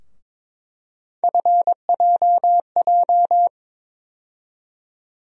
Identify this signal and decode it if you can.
Morse code, heard as a keyed tone, about 22 wpm, 700 Hz. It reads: FJJ
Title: USA Call Sign Postfixes - 3 Letters 22wpm